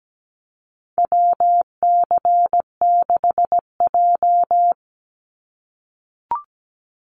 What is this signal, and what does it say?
Morse code, heard as a keyed tone, about 17 wpm, 700 Hz. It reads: WC6J
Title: NA SOTA Chaser Callsigns 17wpm